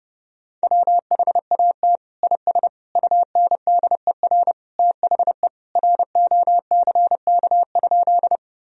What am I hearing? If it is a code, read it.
Morse code, heard as a keyed tone, about 30 wpm, 700 Hz. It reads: WHAT IS UNDER THE ROCK?
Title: Sentences from Top 1000 Words - Review Part 2 - 30wpm